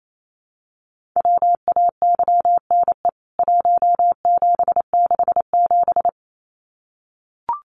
Morse code, heard as a keyed tone, about 28 wpm, 700 Hz. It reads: WAYNE 1767
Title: CWT Exchange 28wpm